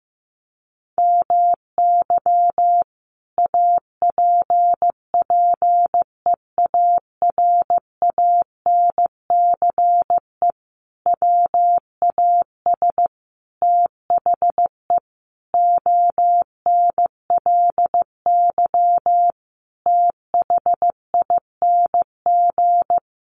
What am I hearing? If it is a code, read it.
Morse code, heard as a keyed tone, about 15 wpm, 700 Hz. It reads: MY APPEARANCE WAS THE ONLY THING
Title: Sentences from Top 1100 Words 15wpm